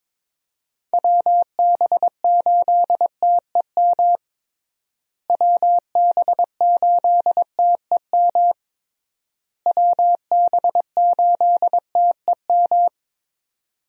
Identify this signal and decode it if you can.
Morse code, heard as a keyed tone, about 21 wpm, 700 Hz. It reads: WB8TEM WB8TEM WB8TEM
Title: USA Call Signs - 6 Characters - Repeated 3x - 22wpm